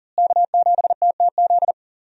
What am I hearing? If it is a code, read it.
Morse code, heard as a keyed tone, about 40 wpm, 700 Hz. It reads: K7TTZ
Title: USA Call Signs - 5 Characters - Repeated 3x - 40wpm